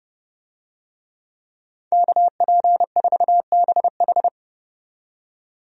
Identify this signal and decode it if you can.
Morse code, heard as a keyed tone, about 30 wpm, 700 Hz. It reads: KP4BH
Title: DXCC-100 Call Signs - 5 Characters - 30wpm